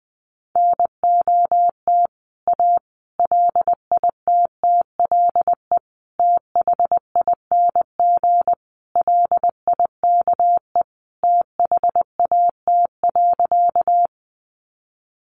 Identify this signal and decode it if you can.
Morse code, heard as a keyed tone, about 20 wpm, 700 Hz. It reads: NOT A LITTLE THING LIKE THAT.
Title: A Sound Of Thunder 20wpm